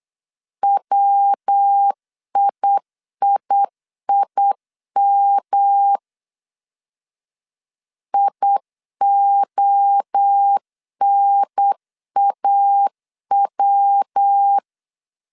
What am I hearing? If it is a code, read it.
Morse code, heard as a keyed tone, about 8 wpm, 800 Hz. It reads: WIIIM IONAW